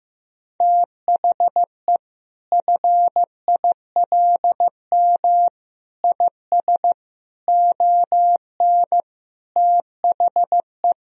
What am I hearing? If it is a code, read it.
Morse code, heard as a keyed tone, about 16 wpm, 700 Hz. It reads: THE FILM IS ON THE
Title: Sentences from Top 1000 Words - Review Part 3 - 15wpm